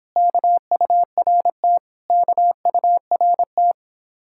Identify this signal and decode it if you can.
Morse code, heard as a keyed tone, about 27 wpm, 700 Hz. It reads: KURT KURT